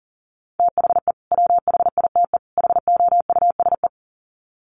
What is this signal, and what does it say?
Morse code, heard as a keyed tone, about 40 wpm, 700 Hz. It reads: THE WHITE HOUSE
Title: Sets of 3 Words 40wpm